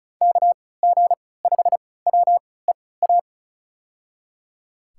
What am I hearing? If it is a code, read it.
Morse code, heard as a keyed tone, about 36 wpm, spaced as for 20 wpm, 700 Hz. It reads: KG5WEA
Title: USA Call Signs - 6 Characters - 3x Character Spacing - 35wpm